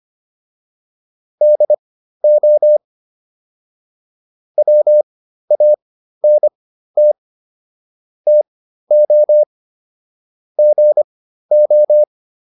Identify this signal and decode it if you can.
Morse code, heard as a keyed tone, about 25 wpm, spaced as for 13 wpm, 600 Hz. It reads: DO WANT TO GO